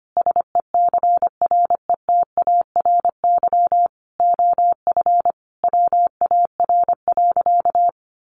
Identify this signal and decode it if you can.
Morse code, heard as a keyed tone, about 25 wpm, 700 Hz. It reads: SECRETARY OF WAR.
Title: Sentences from Top 1300 Words 25wpm